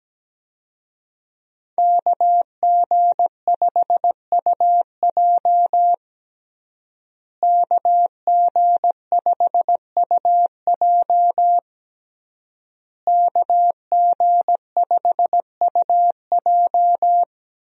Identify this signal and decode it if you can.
Morse code, heard as a keyed tone, about 17 wpm, 700 Hz. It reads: KG5UJ KG5UJ KG5UJ
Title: USA Call Signs - 5 Characters - Repeated 3x - 17wpm